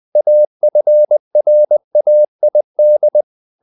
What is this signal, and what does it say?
Morse code, heard as a keyed tone, about 20 wpm, 600 Hz. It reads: AFRAID
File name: the old man and the sea part 2 20wpm